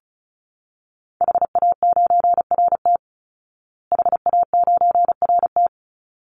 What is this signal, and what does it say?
Morse code, heard as a keyed tone, about 34 wpm, 700 Hz. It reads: HA9RT HA9RT
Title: Call Signs Repeated 3x - 35wpm